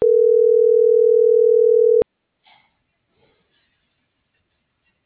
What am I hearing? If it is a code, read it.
Ringback tone (North America), 440 + 480 Hz.